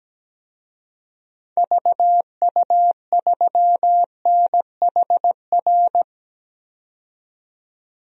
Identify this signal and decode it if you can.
Morse code, heard as a keyed tone, about 17 wpm, 700 Hz. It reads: VU3NHR
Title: DXCC-100 Call Signs - 6 Characters - 17wpm